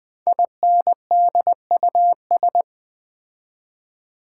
Morse code, heard as a keyed tone, about 20 wpm, 700 Hz. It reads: INDUS
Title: Build A Word - 5 Letters At A Time - 10-Letter Words - 20wpm